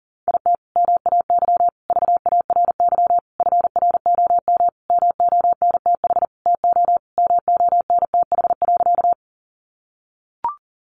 Morse code, heard as a keyed tone, about 40 wpm, 700 Hz. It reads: IT MAY VARY FROM MONTH TO MONTH.
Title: Sentences from Top 1200 Words 40wpm